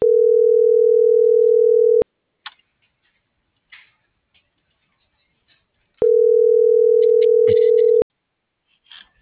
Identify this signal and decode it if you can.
Ringback tone (North America), 440 + 480 Hz.
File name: CENTRAL SAGOT / out anyelo1.sagot - .39507